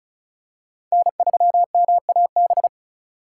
Morse code, heard as a keyed tone, about 36 wpm, 700 Hz. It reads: N3MAB
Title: USA Call Signs - 5 Characters - 3x Character Spacing - 35wpm